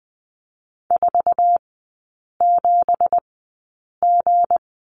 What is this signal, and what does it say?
Morse code, heard as a keyed tone, about 20 wpm, spaced as for 8 wpm, 690 Hz. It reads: 47G
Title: 191 Lesson 31e - Letter v - Three Characters 20wpm